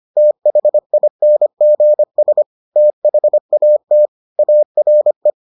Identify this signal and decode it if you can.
Morse code, heard as a keyed tone, about 25 wpm, 600 Hz. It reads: THINGS THAT ARE